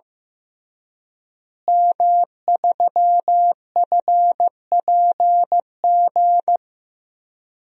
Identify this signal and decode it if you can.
Morse code, heard as a keyed tone, about 15 wpm, 700 Hz. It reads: M3FPG
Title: DXCC-100 Call Signs - 5 Characters - Repeated 3x - 15wpm